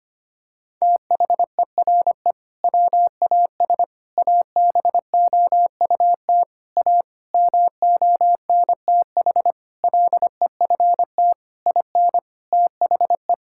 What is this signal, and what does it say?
Morse code, heard as a keyed tone, about 25 wpm, 700 Hz. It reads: THERE WAS ABOUT A MONTH LEFT IN THE
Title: Sentences from Top 1000 Words - Review Part 5 - 25wpm